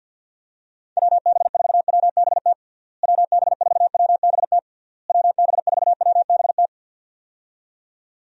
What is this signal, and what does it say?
Morse code, heard as a keyed tone, about 48 wpm, 700 Hz. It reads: WB4WBT WB4WBT WB4WBT
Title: USA Call Signs - 6 Characters - Repeated 3x - 50wpm